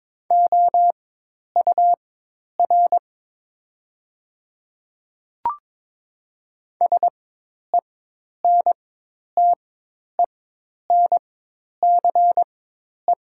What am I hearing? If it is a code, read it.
Morse code, heard as a keyed tone, about 22 wpm, spaced as for 10 wpm, 700 Hz. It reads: OUR SENTENCE
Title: 4x Character Spacing - Top 200 Words - 22wpm